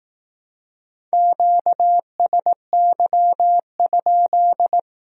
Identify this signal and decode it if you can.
Morse code, heard as a keyed tone, about 18 wpm, 700 Hz. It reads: QSY?
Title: Speed-Racing Q-Codes - 15wpm